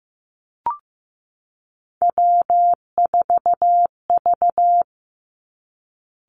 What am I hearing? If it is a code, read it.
Morse code, heard as a keyed tone, about 15 wpm, 700 Hz. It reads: W4V
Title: Three Challenging Letter-Number Combinations 15wpm